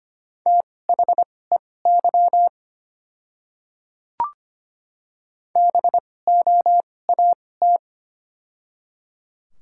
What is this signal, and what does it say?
Morse code, heard as a keyed tone, about 25 wpm, spaced as for 18 wpm, 700 Hz. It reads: THEY BOAT
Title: 2x Character Spacing - Top 500 Words - 25wpm